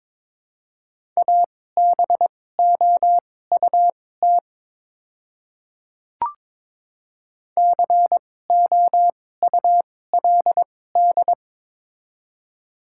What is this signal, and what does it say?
Morse code, heard as a keyed tone, about 22 wpm, spaced as for 16 wpm, 700 Hz. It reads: ABOUT COULD
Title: 2x Character Spacing - Top 100 Words - 22wpm